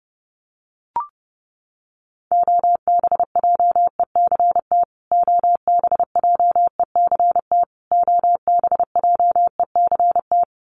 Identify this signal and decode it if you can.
Morse code, heard as a keyed tone, about 28 wpm, 700 Hz. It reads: OBJECT OBJECT OBJECT
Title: Top 500 Words Repeated 30wpm